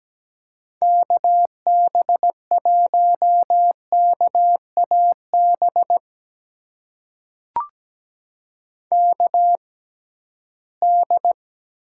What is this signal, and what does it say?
Morse code, heard as a keyed tone, about 17 wpm, spaced as for 13 wpm, 700 Hz. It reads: KB1KAB KD
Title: USA Call Signs - 6 Characters - 6x Character Spacing - 17wpm